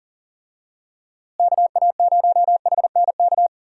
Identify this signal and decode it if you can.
Morse code, heard as a keyed tone, about 40 wpm, 700 Hz. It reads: KA0HNK
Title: USA Call Signs - 6 Characters - Repeated 3x - 40wpm